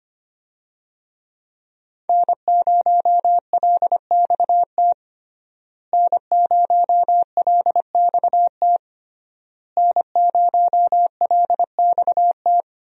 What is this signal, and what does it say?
Morse code, heard as a keyed tone, about 25 wpm, 700 Hz. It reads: N0LXT N0LXT N0LXT
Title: USA Call Signs - 5 Characters - Repeated 3x - 25wpm